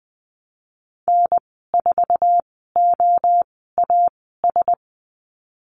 Morse code, heard as a keyed tone, about 20 wpm, spaced as for 15 wpm, 700 Hz. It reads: N4OAS
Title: 048 Lesson 9g - Number 4 - Call Signs 20wpm